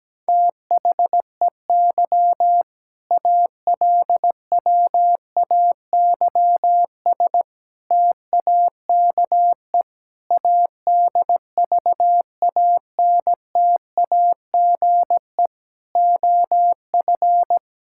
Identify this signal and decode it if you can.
Morse code, heard as a keyed tone, about 17 wpm, 700 Hz. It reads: THEY ALWAYS TAKE ADVANTAGE OF
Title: Sentences from Top 1000 Words - Review Part 6 - 17wpm